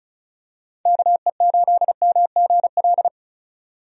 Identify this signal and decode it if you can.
Morse code, heard as a keyed tone, about 36 wpm, 700 Hz. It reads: KE8MGL
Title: USA Call Signs - 6 Characters - 35wpm